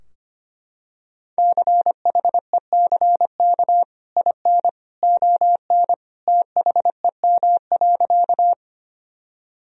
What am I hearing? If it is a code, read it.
Morse code, heard as a keyed tone, about 25 wpm, 700 Hz. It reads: CHECK IN ON THEM.
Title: Sentences from Top 1000 Words - Review Part 3 - 25wpm